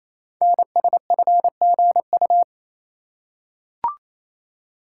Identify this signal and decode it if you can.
Morse code, heard as a keyed tone, about 27 wpm, 700 Hz. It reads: NSFGU
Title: Code Groups - 5 Letters - 28wpm